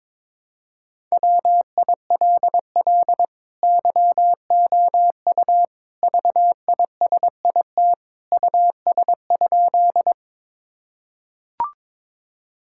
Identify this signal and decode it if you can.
Morse code, heard as a keyed tone, about 22 wpm, 700 Hz. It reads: WILL YOU VISIT US?
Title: Sentences from Top 1000 Words - Review Part 6 - 22wpm